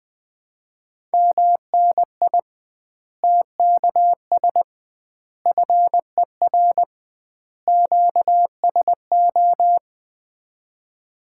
Morse code, heard as a keyed tone, about 20 wpm, 700 Hz. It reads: MNI TKS FER QSO
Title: Common QSO Phrases - 2x Word-Spacing - 20wpm